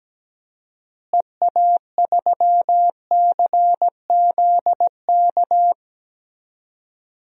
Morse code, heard as a keyed tone, about 17 wpm, 700 Hz. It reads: EA3CZK
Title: DXCC-100 Call Signs - 6 Characters - Repeated 2x - 17wpm